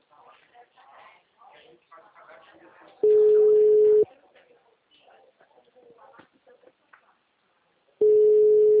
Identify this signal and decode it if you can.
Ringback tone (Europe, 425 Hz), with speech.